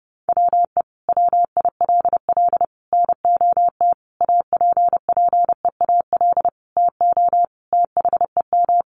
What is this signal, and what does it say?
Morse code, heard as a keyed tone, about 30 wpm, 700 Hz. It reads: WE WILL NOT APPEAL TO THEM
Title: Sentences from Top 1300 Words 30wpm